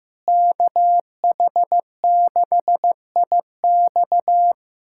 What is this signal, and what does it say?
Morse code, heard as a keyed tone, about 15 wpm, 700 Hz. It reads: KH6IX
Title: USA Callsign DE Callsign - 5 Characters - 15wpm